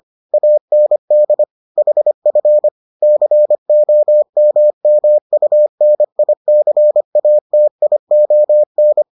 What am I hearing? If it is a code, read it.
Morse code, heard as a keyed tone, about 25 wpm, 600 Hz. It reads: AND HF COMMUNICATION